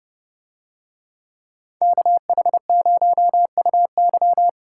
Morse code, heard as a keyed tone, about 28 wpm, 700 Hz. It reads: KH0UY